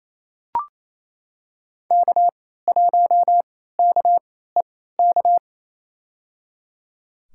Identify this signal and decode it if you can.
Morse code, heard as a keyed tone, about 28 wpm, spaced as for 16 wpm, 700 Hz. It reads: K1KEK
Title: USA Call Signs - 3x Character Spacing - 28wpm